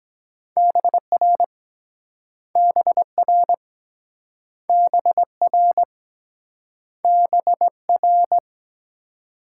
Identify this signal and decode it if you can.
Morse code, heard as a keyed tone, about 23 wpm, 700 Hz. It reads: BR BR BR BR